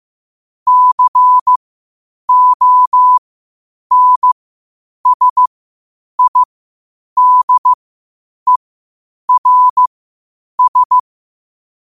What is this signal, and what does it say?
Morse code, heard as a keyed tone, about 15 wpm, spaced as for 8 wpm, 1000 Hz. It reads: CONSIDERS
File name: qotdfiles / 8wpm